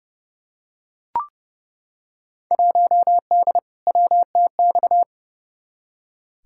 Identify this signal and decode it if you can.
Morse code, heard as a keyed tone, about 30 wpm, 700 Hz. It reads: 1D WTX
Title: Field Day Exchanges - 30wpm